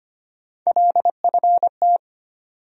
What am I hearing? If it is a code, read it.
Morse code, heard as a keyed tone, about 25 wpm, 700 Hz. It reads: LFT
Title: USA Call Sign Postfixes - 3 Letters 25wpm